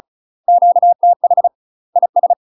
Morse code, heard as a keyed tone, about 36 wpm, 700 Hz. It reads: QTH IS
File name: UK MASTER QSOs TRAINING / 35WPM / UK MASTER QSOs 0041